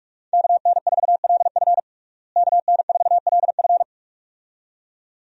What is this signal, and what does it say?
Morse code, heard as a keyed tone, about 41 wpm, 700 Hz. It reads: KN4LF KN4LF
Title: USA Call Signs - 5 Characters - Repeated 2x - 45wpm